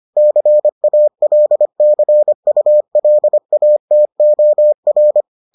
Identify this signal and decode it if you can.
Morse code, heard as a keyed tone, about 25 wpm, 600 Hz. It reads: CALCULATOR